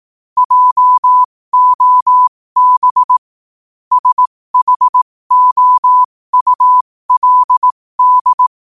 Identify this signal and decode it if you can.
Morse code, heard as a keyed tone, about 18 wpm, spaced as for 15 wpm, 1000 Hz. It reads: JOB SHOULD
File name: qotdfiles / 15wpm